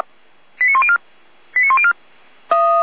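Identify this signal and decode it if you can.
Five-tone selective call (ZVEI), heard as quick runs of five tones, tones of about 70 ms. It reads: 89104 89104
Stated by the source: ZVEI five-tone call received on SDR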